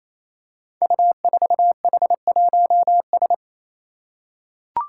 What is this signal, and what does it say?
Morse code, heard as a keyed tone, about 27 wpm, 700 Hz. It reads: U4H1S
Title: Five Challenging Letter-Number Combinations 28wpm